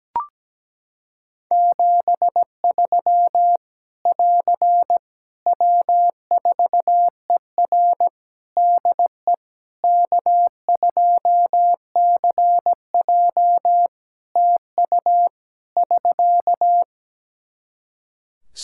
Morse code, heard as a keyed tone, about 17 wpm, 700 Hz. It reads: 73 <AR> W4ER DE K2CJ TU <SK>
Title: Practice QSO - Exchange 4 - Ending - 17wpm